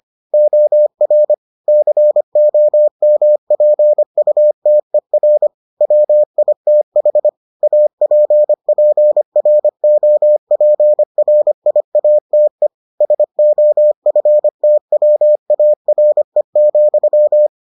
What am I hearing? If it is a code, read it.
Morse code, heard as a keyed tone, about 25 wpm, 600 Hz. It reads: OR COMPUTER WITH APPROPRIATE SOFTWARE,